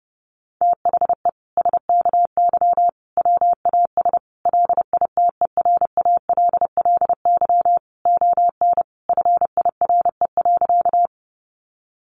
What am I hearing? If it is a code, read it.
Morse code, heard as a keyed tone, about 30 wpm, 700 Hz. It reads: THE SKY WAS LITERALLY ON FIRE.
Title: Sentences from Top 1400 Words 30wpm